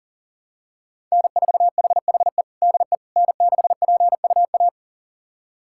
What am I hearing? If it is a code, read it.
Morse code, heard as a keyed tone, about 40 wpm, 700 Hz. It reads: N4HHE DE N6PUA
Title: USA Callsign DE Callsign - 5 Characters - 40wpm